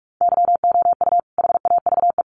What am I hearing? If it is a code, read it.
Morse code, heard as a keyed tone, about 44 wpm, 700 Hz. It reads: YOU HAVE
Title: Sentences from Top 100 Words 45wpm